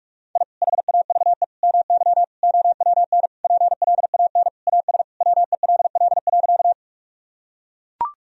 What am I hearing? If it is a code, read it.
Morse code, heard as a keyed tone, about 42 wpm, 700 Hz. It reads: I HAVE MY OWN PLAN AS WELL.
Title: Sentences from Top 1000 Words - Review Part 2 - 45wpm